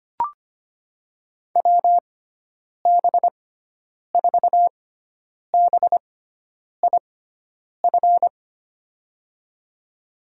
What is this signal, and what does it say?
Morse code, heard as a keyed tone, about 25 wpm, spaced as for 9 wpm, 700 Hz. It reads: WB4BIF
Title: USA Call Signs - 6 Characters - 6x Character Spacing - 25wpm